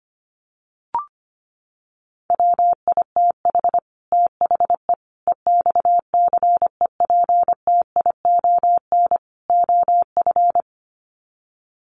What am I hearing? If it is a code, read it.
Morse code, heard as a keyed tone, about 25 wpm, 700 Hz. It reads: WITH THE EXCEPTION OF
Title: Sets of 4 Words 25wpm